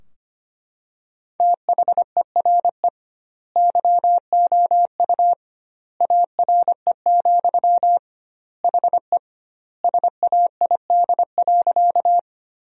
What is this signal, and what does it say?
Morse code, heard as a keyed tone, about 25 wpm, 700 Hz. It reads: THERE YOU ARE, HE SAID.
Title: 2x Word-Spacing - Sentences from Top 100 Words - 25wpm